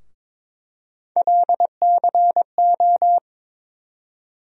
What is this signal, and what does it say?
Morse code, heard as a keyed tone, about 22 wpm, 700 Hz. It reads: LCO
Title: USA Call Sign Postfixes - 3 Letters 22wpm